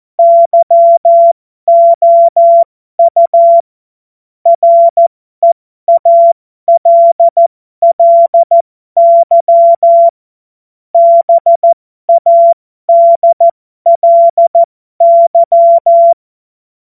Morse code, heard as a keyed tone, about 14 wpm, spaced as for 12 wpm, 680 Hz. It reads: YOU REALLY BADLY
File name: Morsekurs / Witze